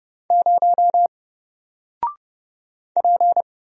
Morse code, heard as a keyed tone, about 31 wpm, 700 Hz. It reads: 0 P
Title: Single Character (Letter:Number:Punctuation:Prosign) Alt 30wpm